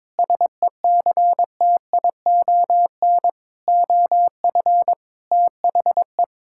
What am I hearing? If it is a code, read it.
Morse code, heard as a keyed tone, about 22 wpm, 700 Hz. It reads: SECTION OF THE
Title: Sentences from Top 1000 Words - Review Part 5 - 22wpm